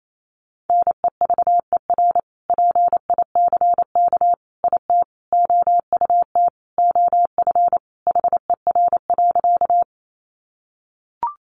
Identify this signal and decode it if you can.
Morse code, heard as a keyed tone, about 28 wpm, 700 Hz. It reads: NEVER PICK IT OUT OF HER.
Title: Sentences from Top 1000 Words 28wpm